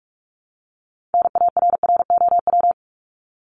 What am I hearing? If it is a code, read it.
Morse code, heard as a keyed tone, about 44 wpm, 700 Hz. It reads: NARROW
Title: Top 1501-1750 Words 45wpm